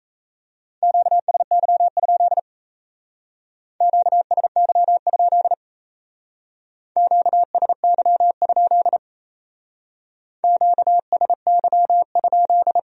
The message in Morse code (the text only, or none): QSY? QSY? QSY? QSY?